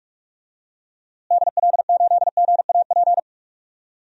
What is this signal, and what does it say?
Morse code, heard as a keyed tone, about 43 wpm, 700 Hz. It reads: DL8GAP
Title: DXCC-100 Call Signs - 6 Characters - Repeated 2x - 45wpm